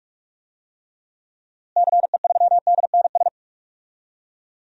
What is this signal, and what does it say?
Morse code, heard as a keyed tone, about 46 wpm, 700 Hz. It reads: CE3DNS
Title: DXCC-100 Call Signs - 6 Characters - Repeated 2x - 45wpm